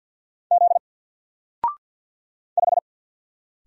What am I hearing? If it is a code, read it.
Morse code, heard as a keyed tone, about 50 wpm, 700 Hz. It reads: Z 5
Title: Single Character (Letter:Number:Punctuation:Prosign) 50wpm